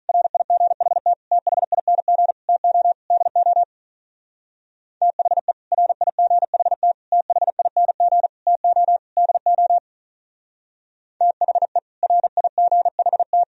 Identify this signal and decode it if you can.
Morse code, heard as a keyed tone, about 43 wpm, 700 Hz. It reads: RIGHT THING TO DO THE RIGHT THING TO DO THE RIGHT
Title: Speed-Racing - Sets of 5 Words 35wpm